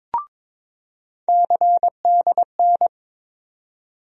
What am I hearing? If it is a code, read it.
Morse code, heard as a keyed tone, about 22 wpm, 700 Hz. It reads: CDN
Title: USA Call Sign Postfixes - 3 Letters 22wpm